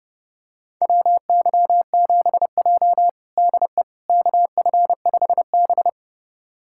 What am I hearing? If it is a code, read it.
Morse code, heard as a keyed tone, about 30 wpm, 700 Hz. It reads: WY7J DE KF5B
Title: USA Callsign DE Callsign - 4 Characters - 30wpm